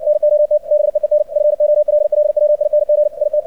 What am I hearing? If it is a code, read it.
Morse code, heard as a keyed tone, about 56 wpm, 600 Hz, with radio noise. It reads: NOT BEEN FOLLWING IT
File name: archived / CRSnet